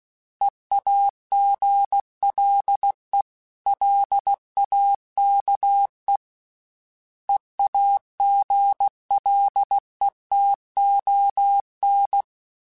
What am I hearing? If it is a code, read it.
Morse code, heard as a keyed tone, about 16 wpm, 800 Hz. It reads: EAGLELAKE EAGLETON
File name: fromSMs / FromSMs mp3s / Cities@15